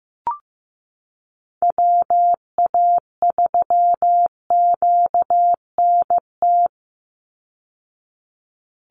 Morse code, heard as a keyed tone, about 15 wpm, 700 Hz. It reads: WA3QNT
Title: Call Signs Repeated 2x - 15wpm